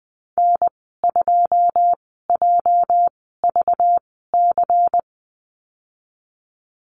Morse code, heard as a keyed tone, about 20 wpm, spaced as for 14 wpm, 700 Hz. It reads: N2JVC
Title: 219 Lesson 35g - Number 0 - Call Signs 20wpm